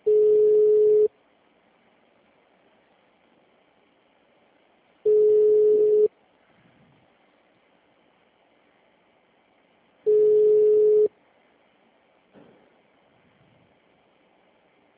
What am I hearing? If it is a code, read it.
Ringback tone (Europe, 425 Hz).
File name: out- -2006- - - .378